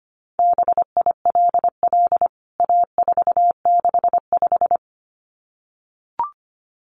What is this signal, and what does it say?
Morse code, heard as a keyed tone, about 25 wpm, 700 Hz. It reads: BILL A465
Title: CWT Exchange 25wpm